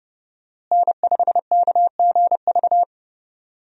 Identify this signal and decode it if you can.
Morse code, heard as a keyed tone, about 30 wpm, 700 Hz. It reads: N5KGV
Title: USA Call Signs - 5 Characters - Repeated 2x - 30wpm